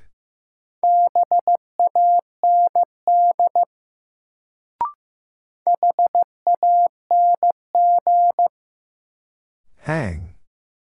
Morse code, heard as a keyed tone, about 15 wpm, 700 Hz. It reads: BAND HANG
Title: Sentences from Top 1600 Words - New Word Drill Alt - 15wpm